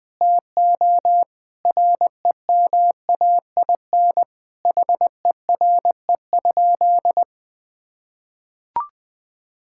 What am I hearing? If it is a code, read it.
Morse code, heard as a keyed tone, about 21 wpm, 700 Hz. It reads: TO REMAIN HERE?
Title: Sentences from Top 1600 Words 20wpm